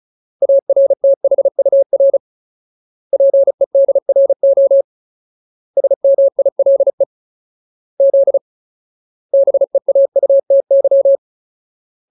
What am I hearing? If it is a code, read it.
Morse code, heard as a keyed tone, about 34 wpm, 550 Hz. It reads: ARTHUR PEDRO SMILE Z BEAUTY
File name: Ultimate Word List / 35WPM Ultimate